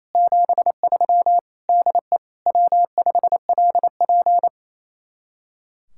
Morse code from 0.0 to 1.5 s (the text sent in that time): 73